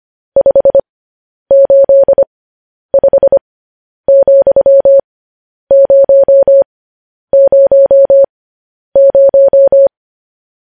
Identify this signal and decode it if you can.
Morse code, heard as a keyed tone, about 25 wpm, spaced as for 10 wpm, 550 Hz. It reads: 585,000
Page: news-headlines-in-morse-code-at-10-wpm